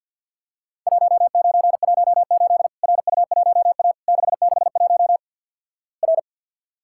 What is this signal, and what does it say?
Morse code, heard as a keyed tone, about 48 wpm, 700 Hz. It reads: 1918 RU1A 661 R
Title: Worked All Europe DX Contest - QTC Exchange with Repeats - 50wpm